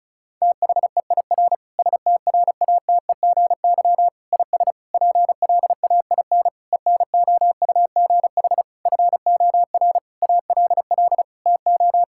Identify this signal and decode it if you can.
Morse code, heard as a keyed tone, about 36 wpm, 700 Hz. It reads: THEIR STRATEGY IS PLAIN ENOUGH FOR ALL TO
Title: Sentences from Top 1000 Words - Review Part 5 - 35wpm